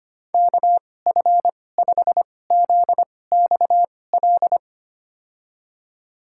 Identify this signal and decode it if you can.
Morse code, heard as a keyed tone, about 25 wpm, spaced as for 18 wpm, 700 Hz. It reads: KF5ZXL
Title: USA Call Signs - 2x Character Spacing - 25wpm